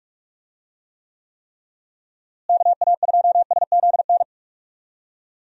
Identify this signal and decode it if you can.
Morse code, heard as a keyed tone, about 41 wpm, 700 Hz. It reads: KA2SZN
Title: USA Call Signs - 6 Characters - Repeated 3x - 45wpm